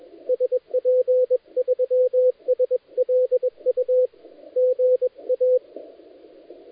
Morse code, heard as a keyed tone, about 23 wpm, 510 Hz, with radio noise. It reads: SP3SLU GA
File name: dlazaawansowanych / nagraniazpasma / titawaNETcontrol